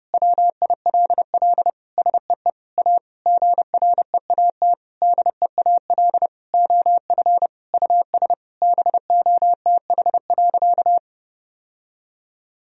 Morse code, heard as a keyed tone, about 32 wpm, 700 Hz. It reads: WILL SEE A GREAT DEAL OF US BOTH.
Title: Sentences from Top 1000 Words - Review Part 4 - 30wpm